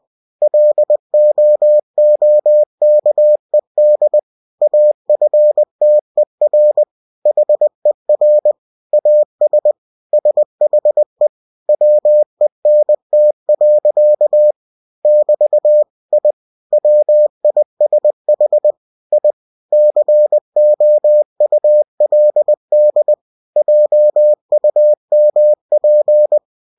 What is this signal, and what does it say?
Morse code, heard as a keyed tone, about 20 wpm, 600 Hz. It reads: LOOKED AFTER HER AS SHE WENT. <BT> I WISH I COULD JUMP
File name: cw-books / winnie-the-pooh / winnie-20wpm-page-0070